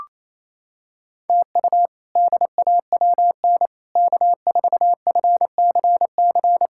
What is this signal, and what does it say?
Morse code, heard as a keyed tone, about 28 wpm, 700 Hz. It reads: TU DAWN K4FCC
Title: K1USN SST QSO Practice - 28wpm